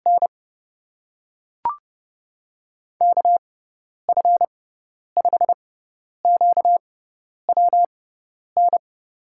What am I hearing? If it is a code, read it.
Morse code, heard as a keyed tone, about 30 wpm, spaced as for 10 wpm, 700 Hz. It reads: N KF5QWN
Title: USA Call Signs - 6 Characters - 6x Character Spacing - 30wpm